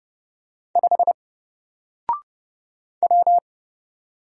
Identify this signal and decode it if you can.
Morse code, heard as a keyed tone, about 30 wpm, 700 Hz. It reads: 5 W
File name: Single-Letter-Number-30wpm